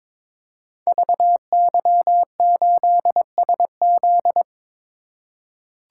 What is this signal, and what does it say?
Morse code, heard as a keyed tone, about 23 wpm, 700 Hz. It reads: VY8SZ
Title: Code Groups - 5 Letter:Numbers - 22wpm